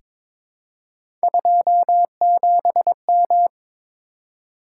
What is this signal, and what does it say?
Morse code, heard as a keyed tone, about 22 wpm, 700 Hz. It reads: 27M
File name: practice / 22wpm / Three Letter-Number Combinations - 22wpm